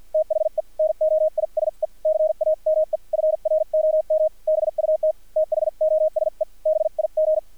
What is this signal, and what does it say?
Morse code, heard as a keyed tone, about 50 wpm, spaced as for 43 wpm, 630 Hz. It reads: THE NOISE CAME FROM BUT THOSE BIG
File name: archived / CRSnet